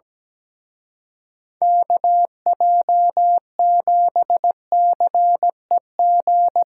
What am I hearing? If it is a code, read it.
Morse code, heard as a keyed tone, about 19 wpm, 700 Hz. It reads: KJ7CEG
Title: Speed Racing - USA Call Signs 17wpm